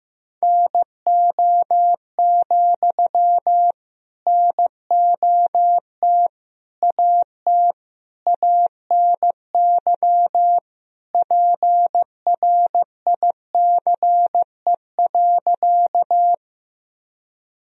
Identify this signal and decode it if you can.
Morse code, heard as a keyed tone, about 15 wpm, 700 Hz. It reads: NO, NOT AT ANY PRICE.
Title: Sentences from Top 1000 Words - Review Part 4 - 15wpm